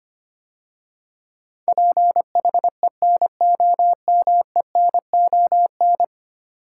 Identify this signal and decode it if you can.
Morse code, heard as a keyed tone, about 25 wpm, 700 Hz. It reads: PHENOMENON
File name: 10 Letter Words - First 3 Letters Given - 25wpm